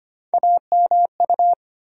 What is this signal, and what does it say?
Morse code, heard as a keyed tone, about 25 wpm, 700 Hz. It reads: AMU
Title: USA Call Sign Postfixes - 3 Letters 25wpm